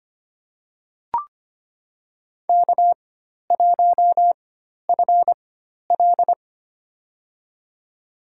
Morse code, heard as a keyed tone, about 25 wpm, spaced as for 12 wpm, 700 Hz. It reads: K1FL
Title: USA Call Signs - 4x Character Spacing - 25wpm